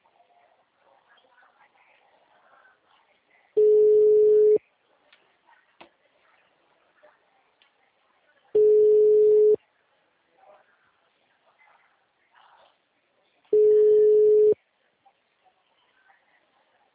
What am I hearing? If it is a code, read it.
Ringback tone (Europe, 425 Hz), with speech.